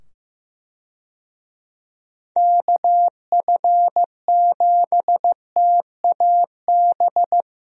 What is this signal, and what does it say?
Morse code, heard as a keyed tone, about 16 wpm, 700 Hz. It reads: KF7TAB
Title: USA Call Signs - 6 Characters - 15wpm